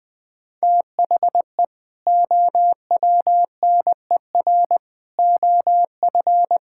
Morse code, heard as a keyed tone, about 20 wpm, 700 Hz. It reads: THE OWNER OF
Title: Sentences with 3-Word Phrases - Spoken Beginning - 20wpm